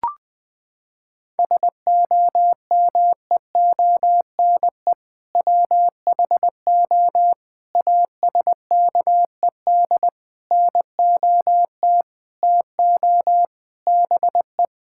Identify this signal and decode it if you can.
Morse code, heard as a keyed tone, about 20 wpm, 700 Hz. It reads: SOMEONE WHO ASKED NOT TO BE
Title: Sentences with 7-Word Phrases - 20wpm